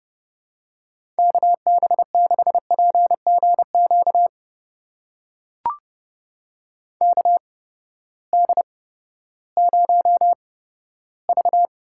Morse code, heard as a keyed tone, about 30 wpm, spaced as for 15 wpm, 700 Hz. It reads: KB6PGQ KD0V
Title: USA Call Signs - 6 Characters - 8x Character Spacing - 30wpm